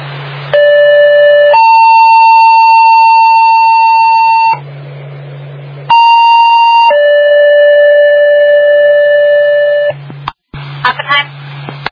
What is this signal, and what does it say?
Two-tone page, heard as a one-second tone, then a three-second tone, about 580 then 900 Hz, about 930 then 580 Hz, with speech. It is heard from 0.5 to 9.9 s.